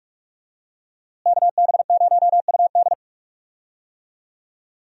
Morse code, heard as a keyed tone, about 46 wpm, 700 Hz. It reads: KB0UD
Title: USA Call Signs - 5 Characters - Repeated 3x - 45wpm